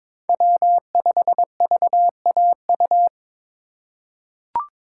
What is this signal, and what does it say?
Morse code, heard as a keyed tone, about 22 wpm, 700 Hz. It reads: W5VAU
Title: USA Call Signs - 2x Character Spacing - 22wpm